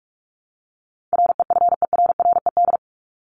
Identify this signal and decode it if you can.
Morse code, heard as a keyed tone, about 44 wpm, 700 Hz. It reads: REFERRED
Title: ED Suffix - Top 100 Words with Base Words 45wpm